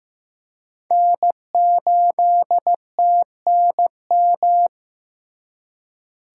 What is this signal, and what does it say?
Morse code, heard as a keyed tone, about 18 wpm, 700 Hz. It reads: N8TNM
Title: Speed Racing - USA 1x3 Call Signs 15wpm